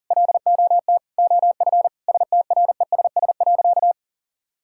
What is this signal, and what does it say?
Morse code, heard as a keyed tone, about 40 wpm, 700 Hz. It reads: LOT OF STRESS.
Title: Sentences from Top 1000 Words - Review Part 6 - 40wpm